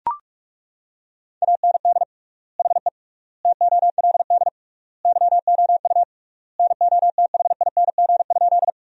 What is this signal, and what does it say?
Morse code, heard as a keyed tone, about 41 wpm, 700 Hz. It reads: AND HE TOLD YOU NOTHING?
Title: 3x Word-Spacing - Sentences from Top 500 Words - 45wpm